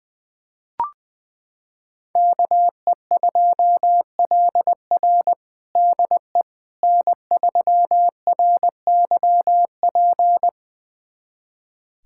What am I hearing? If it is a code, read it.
Morse code, heard as a keyed tone, about 20 wpm, 700 Hz. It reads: KE2LR DE N3RYP
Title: USA Callsign DE Callsign - 5 Characters - 20wpm